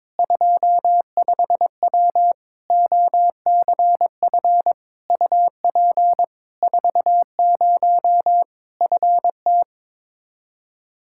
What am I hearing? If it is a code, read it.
Morse code, heard as a keyed tone, about 22 wpm, 700 Hz. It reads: 25W OCF UP 40 FT
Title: Practice QSO - Exchange 2 - Info - 22wpm